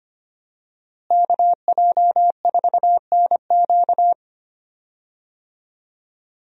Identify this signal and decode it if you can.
Morse code, heard as a keyed tone, about 25 wpm, 700 Hz. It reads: KJ4NQ
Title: USA Call Signs - 5 Characters - Repeated 3x - 25wpm